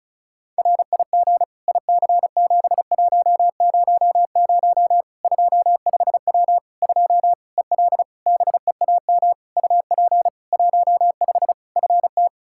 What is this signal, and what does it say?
Morse code, heard as a keyed tone, about 34 wpm, 700 Hz. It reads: RIG IC7100 25W 2 EL BEAM UP 15 FT
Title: K1USN SST QSO Practice 35wpm